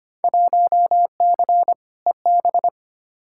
Morse code, heard as a keyed tone, about 25 wpm, 700 Hz. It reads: 1C EB
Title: Field Day Exchanges Encore - 25wpm